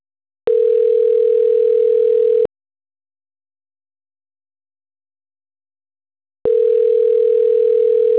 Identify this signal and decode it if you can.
Ringback tone (North America), 440 + 480 Hz.